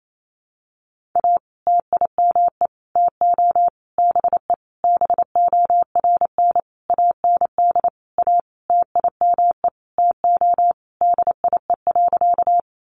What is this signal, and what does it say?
Morse code, heard as a keyed tone, about 28 wpm, 700 Hz. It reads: A TIME TO BE BORN AND A TIME TO DIE.
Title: Sentences from Top 1300 Words 28wpm